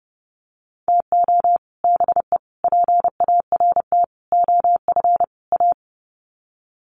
Morse code, heard as a keyed tone, about 30 wpm, 700 Hz. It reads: TO BE PART OF A
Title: Sets of 5 Words Encore2 30wpm